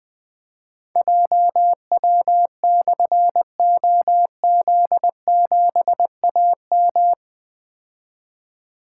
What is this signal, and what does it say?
Morse code, heard as a keyed tone, about 20 wpm, 700 Hz. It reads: JW/OZ7AM
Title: Difficult Call Signs 20wpm